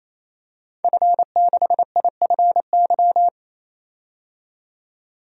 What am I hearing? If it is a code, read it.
Morse code, heard as a keyed tone, about 29 wpm, 690 Hz. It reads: F6IFY
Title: DXCC-100 Call Signs - 5 Characters - 28wpm